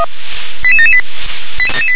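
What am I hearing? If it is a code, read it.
Five-tone selective call (ZVEI), heard as quick runs of five tones, tones of about 70 ms. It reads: 88778
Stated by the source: ZVEI five-tone selective call received by SDR.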